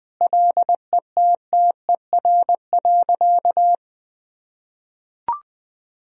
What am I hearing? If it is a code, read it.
Morse code, heard as a keyed tone, about 20 wpm, 700 Hz. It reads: LETTER.
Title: Sentences from Top 1000 Words - Review Part 1 - 20wpm